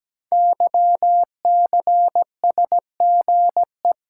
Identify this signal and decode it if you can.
Morse code, heard as a keyed tone, about 17 wpm, 700 Hz. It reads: YCSGE
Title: Code Groups - 5 Letters - 17wpm